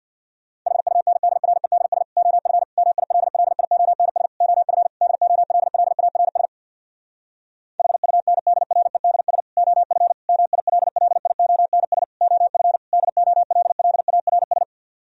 Morse code, heard as a keyed tone, about 64 wpm, 700 Hz. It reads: HUNDREDS OF MILLIONS OF DOLLARS HUNDREDS OF MILLIONS OF DOLLARS